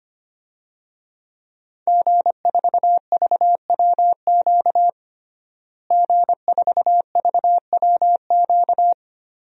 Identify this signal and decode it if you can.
Morse code, heard as a keyed tone, about 25 wpm, 700 Hz. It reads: G4VWQ G4VWQ
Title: DXCC-100 Call Signs - 5 Characters - Repeated 2x - 25wpm